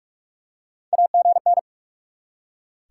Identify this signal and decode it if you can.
Morse code, heard as a keyed tone, about 44 wpm, 700 Hz. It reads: AGN
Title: Top 90 QSO Elements 45wpm